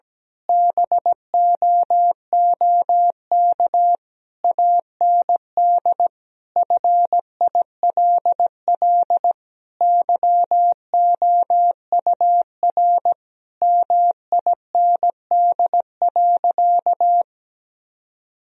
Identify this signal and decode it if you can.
Morse code, heard as a keyed tone, about 17 wpm, 700 Hz. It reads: BOOK AND FILL YOUR MIND.
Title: Sentences from Top 1000 Words - Review Part 3 - 17wpm